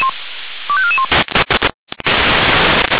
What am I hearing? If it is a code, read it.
Five-tone selective call (ZVEI), heard as quick runs of five tones, tones of about 70 ms. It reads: 25441